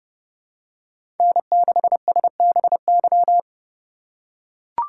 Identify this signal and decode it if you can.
Morse code, heard as a keyed tone, about 30 wpm, 700 Hz. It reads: N6SBY
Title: USA Call Signs - 5 Characters - 8x Character Spacing - 30wpm